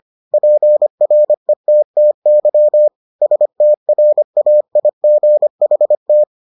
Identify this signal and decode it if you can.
Morse code, heard as a keyed tone, about 25 wpm, 600 Hz. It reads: PRETTY STRAIGHT